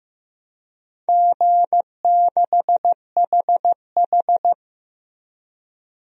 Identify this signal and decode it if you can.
Morse code, heard as a keyed tone, about 15 wpm, 700 Hz. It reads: G6HH
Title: DXCC-100 Call Signs - 4 Characters - Repeated 3x - 15wpm